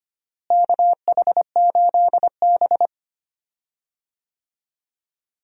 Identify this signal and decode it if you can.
Morse code, heard as a keyed tone, about 25 wpm, 700 Hz. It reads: KH8B
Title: USA Call Signs - 4 Characters - Repeated 2x - 25wpm